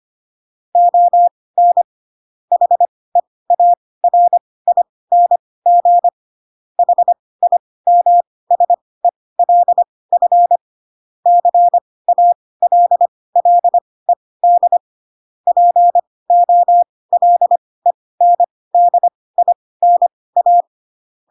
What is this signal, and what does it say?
Morse code, heard as a keyed tone, about 25 wpm, spaced as for 18 wpm, 700 Hz. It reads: ON HEARING HIMSELF CALLED POLENDINA
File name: Pinocchio / Pinocchio ch2 12 25 ik1hge